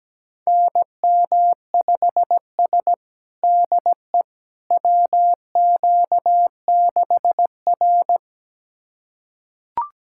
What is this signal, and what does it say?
Morse code, heard as a keyed tone, about 17 wpm, 700 Hz. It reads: NM5S DE WQ6R
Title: USA Callsign DE Callsign - 4 Characters - 17wpm